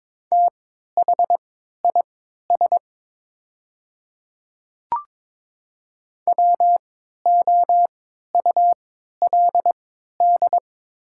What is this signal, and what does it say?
Morse code, heard as a keyed tone, about 22 wpm, spaced as for 12 wpm, 700 Hz. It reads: THIS WOULD
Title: 3x Character Spacing - Top 100 Words - 22wpm